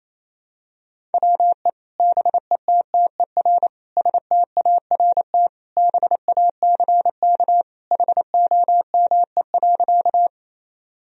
Morse code, heard as a keyed tone, about 28 wpm, 700 Hz. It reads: WE BETTER START BACK HOME.
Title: Sentences from Top 1000 Words - Review Part 1 - 28wpm